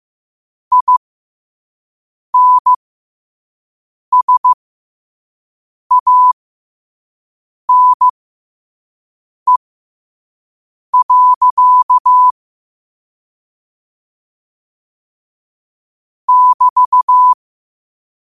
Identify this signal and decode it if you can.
Morse code, heard as a keyed tone, about 15 wpm, spaced as for 5 wpm, 1000 Hz. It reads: INSANE. <BT>
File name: qotdfiles / 5wpm